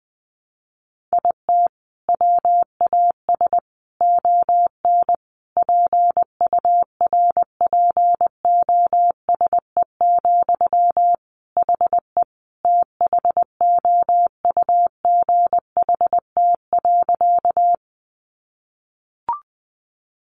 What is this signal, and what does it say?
Morse code, heard as a keyed tone, about 20 wpm, 700 Hz. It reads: IT WAS ON PURPOSE, HE THOUGHT.
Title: Sentences from Top 800 Words 20wpm